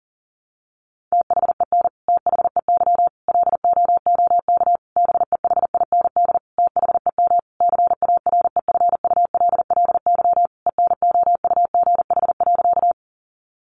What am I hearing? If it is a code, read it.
Morse code, heard as a keyed tone, about 40 wpm, 700 Hz. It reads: THEN THEY LOOK BEHIND THEM CAREFULLY ENOUGH.
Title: Sentences from Top 1300 Words 40wpm